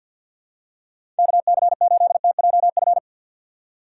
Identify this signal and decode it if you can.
Morse code, heard as a keyed tone, about 53 wpm, 700 Hz. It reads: KC8TJF
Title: USA Call Signs - 6 Characters - 6x Character Spacing - 50wpm